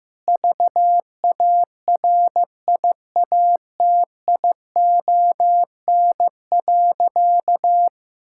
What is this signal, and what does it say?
Morse code, heard as a keyed tone, about 15 wpm, 700 Hz. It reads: VARIATION.
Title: Sentences from Top 1600 Words 15wpm